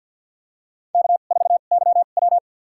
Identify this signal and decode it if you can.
Morse code, heard as a keyed tone, about 53 wpm, spaced as for 37 wpm, 700 Hz. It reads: K4YW
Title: USA Call Signs - 4 Characters - 2x Character Spacing - 50wpm